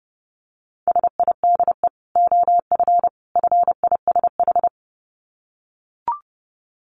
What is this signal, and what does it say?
Morse code, heard as a keyed tone, about 30 wpm, 700 Hz. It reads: SIDE OF FISH
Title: Dit Challenge 30wpm